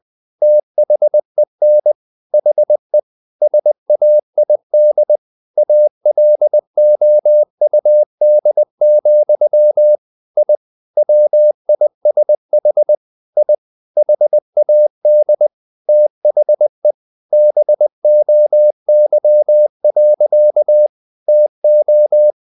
Morse code, heard as a keyed tone, about 20 wpm, 600 Hz. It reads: THEN HE SAID ALOUD, I WISH I HAD THE BOY. TO
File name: the old man and the sea / the old man and the sea part 1 20wpm